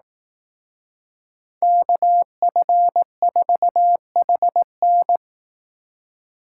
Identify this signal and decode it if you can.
Morse code, heard as a keyed tone, about 18 wpm, 700 Hz. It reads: KF4HN
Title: Speed Racing - USA Call Signs 15wpm